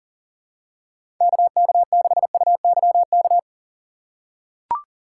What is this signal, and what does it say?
Morse code, heard as a keyed tone, about 40 wpm, 700 Hz. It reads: KK6UYK
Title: USA Call Signs - 2x Character Spacing - 40wpm